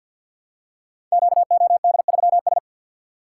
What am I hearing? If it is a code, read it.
Morse code, heard as a keyed tone, about 46 wpm, 700 Hz. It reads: QOD3S
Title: Code Groups - 5 Letter:Numbers - 50wpm